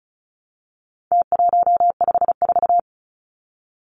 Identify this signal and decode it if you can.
Morse code, heard as a keyed tone, about 34 wpm, 700 Hz. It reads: T154
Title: Numbers - Contest Style - 4-Digit 35wpm